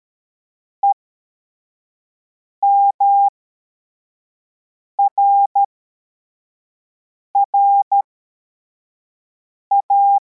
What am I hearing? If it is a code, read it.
Morse code, heard as a keyed tone, about 13 wpm, spaced as for 4 wpm, 800 Hz. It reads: EMRRA
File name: formation / cw / Text25